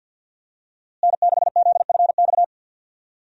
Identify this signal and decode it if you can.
Morse code, heard as a keyed tone, about 46 wpm, 700 Hz. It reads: N6ZFX
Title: USA Call Signs - 5 Characters - Repeated 3x - 50wpm